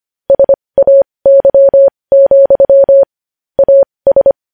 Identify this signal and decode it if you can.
Morse code, heard as a keyed tone, about 25 wpm, spaced as for 20 wpm, 550 Hz. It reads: SAY, AS
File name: podcast / News headlines-1x-25x20